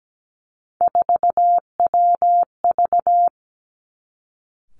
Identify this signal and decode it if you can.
Morse code, heard as a keyed tone, about 17 wpm, 700 Hz. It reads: 4WV
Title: Three Challenging Letter-Number Combinations 17wpm